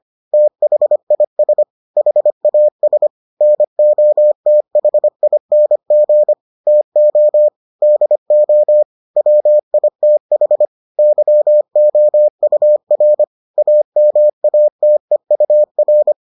Morse code, heard as a keyed tone, about 25 wpm, 600 Hz. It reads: THIS HAS NOTHING TO DO WITH YOUR AMATEUR